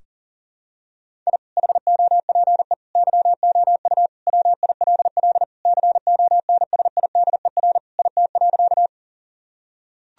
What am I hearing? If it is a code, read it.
Morse code, heard as a keyed tone, about 40 wpm, 700 Hz. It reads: I HOPE YOU WILL CONSIDER IT.
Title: Sentences from Top 1000 Words - Review Part 4 - 40wpm